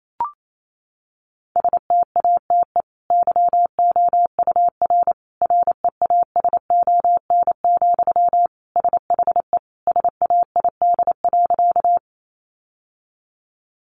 Morse code, heard as a keyed tone, about 28 wpm, 700 Hz. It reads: STATE YOUR REASON, SHE SAID.
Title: Sentences from Top 700 Words 28wpm